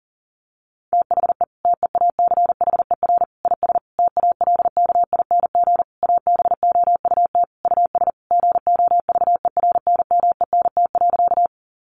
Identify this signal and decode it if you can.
Morse code, heard as a keyed tone, about 37 wpm, 700 Hz. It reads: THE TEACHER IS TALKING ABOUT US GOVERNMENT.
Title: Easy Phrases 40wpm